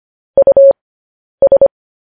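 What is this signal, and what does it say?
Morse code, heard as a keyed tone, about 25 wpm, spaced as for 10 wpm, 550 Hz. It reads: US
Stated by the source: Morse code transcription of news headlines sent at 10 wpm.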